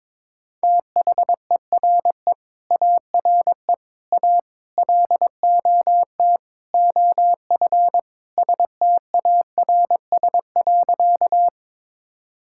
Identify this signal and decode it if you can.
Morse code, heard as a keyed tone, about 22 wpm, 700 Hz. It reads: THERE ARE A LOT OF STARS.
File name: practice / 22wpm / Sentences with 4-Word Phrases - 22wpm